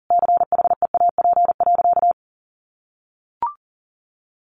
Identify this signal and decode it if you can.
Morse code, heard as a keyed tone, about 40 wpm, 700 Hz. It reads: CHEAP.
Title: Sentences from Top 1200 Words 40wpm